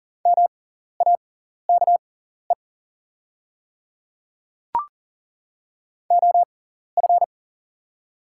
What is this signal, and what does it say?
Morse code, heard as a keyed tone, about 40 wpm, spaced as for 14 wpm, 700 Hz. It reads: MAKE OF
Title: 6x Character Spacing - Top 200 Words - 40wpm